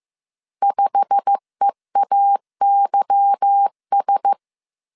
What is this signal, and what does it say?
Morse code, heard as a keyed tone, about 15 wpm, 800 Hz. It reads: 5EAYS
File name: CW / DET / 1 9 1500C S A 14wpm 3222